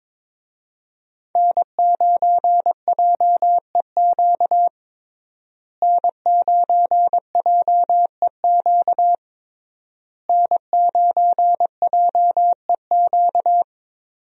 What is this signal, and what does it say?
Morse code, heard as a keyed tone, about 22 wpm, 700 Hz. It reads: N9JEQ N9JEQ N9JEQ
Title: USA Call Signs - 5 Characters - Repeated 3x - 22wpm